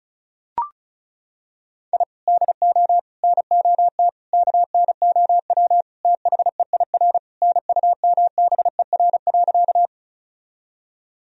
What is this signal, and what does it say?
Morse code, heard as a keyed tone, about 36 wpm, 700 Hz. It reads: I DO NOT KNOW THEIR NUMBER.
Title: Sentences from Top 1000 Words - Review Part 1 - 35wpm